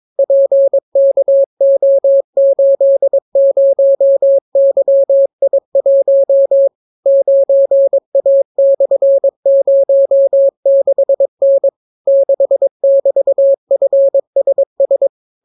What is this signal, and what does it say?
Morse code, heard as a keyed tone, about 22 wpm, 550 Hz. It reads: PKO80YI1 9A/06N 6<BT>FSS